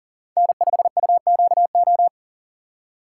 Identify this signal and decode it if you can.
Morse code, heard as a keyed tone, about 37 wpm, 700 Hz. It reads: N5UQO
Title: USA Call Signs - 2x Character Spacing - 40wpm